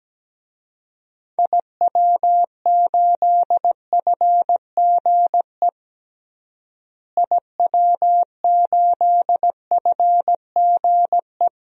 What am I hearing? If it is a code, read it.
Morse code, heard as a keyed tone, about 17 wpm, 700 Hz. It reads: IW8FGE IW8FGE
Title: DXCC-100 Call Signs - 6 Characters - Repeated 2x - 17wpm